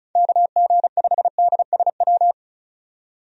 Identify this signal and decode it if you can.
Morse code, heard as a keyed tone, about 36 wpm, 700 Hz. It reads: KG5DSW
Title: USA Call Signs - 6 Characters - 35wpm